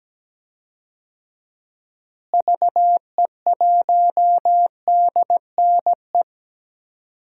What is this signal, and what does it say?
Morse code, heard as a keyed tone, about 17 wpm, 700 Hz. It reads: VE1DNE
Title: DXCC-100 Call Signs - 6 Characters - Repeated 2x - 17wpm